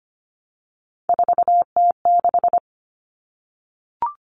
Morse code, heard as a keyed tone, about 25 wpm, 700 Hz. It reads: 4T6
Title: Common Cut Numbers - 3-Digit 25wpm